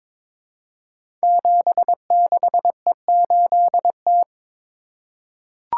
Morse code, heard as a keyed tone, about 22 wpm, 700 Hz. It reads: 76E8T
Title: Code Groups - 5 Letter:Numbers - 22wpm